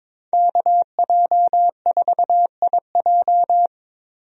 Morse code, heard as a keyed tone, about 22 wpm, 700 Hz. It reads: KJ4IJ
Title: USA Call Signs - 5 Characters - 22wpm